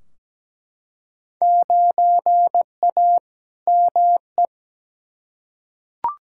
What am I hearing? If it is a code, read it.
Morse code, heard as a keyed tone, about 17 wpm, 700 Hz. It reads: 9A ME
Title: Field Day Exchanges Encore - 17wpm